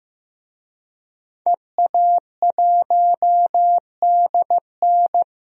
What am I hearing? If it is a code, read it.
Morse code, heard as a keyed tone, about 15 wpm, 700 Hz. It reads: EA1DN
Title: DXCC-100 Call Signs - 5 Characters - 15wpm